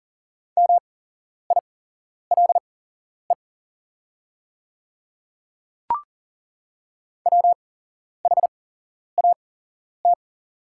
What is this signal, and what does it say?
Morse code, heard as a keyed tone, about 40 wpm, spaced as for 11 wpm, 700 Hz. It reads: MILE WHAT
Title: 8x Character Spacing - Top 300 Words - 40wpm